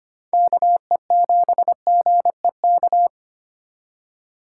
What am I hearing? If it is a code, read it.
Morse code, heard as a keyed tone, about 31 wpm, 700 Hz. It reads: KE7GEK
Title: Speed Racing - USA Call Signs 25wpm